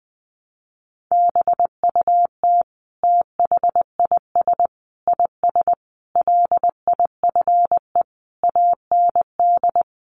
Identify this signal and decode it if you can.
Morse code, heard as a keyed tone, about 20 wpm, 700 Hz. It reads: BUT THIS IS LIFE AND
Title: Sentences from Top 1100 Words 20wpm